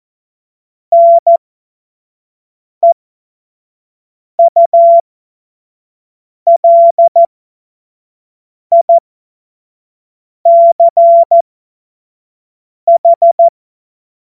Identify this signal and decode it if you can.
Morse code, heard as a keyed tone, about 14 wpm, spaced as for 5 wpm, 680 Hz. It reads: NEULICH
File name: Morsekurs / Witze / Witz 27. April 2021, 25 F70